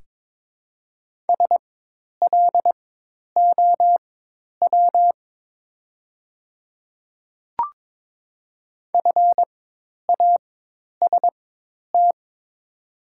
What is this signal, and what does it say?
Morse code, heard as a keyed tone, about 22 wpm, spaced as for 10 wpm, 700 Hz. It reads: SLOW FAST
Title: 4x Character Spacing - Top 500 Words - 22wpm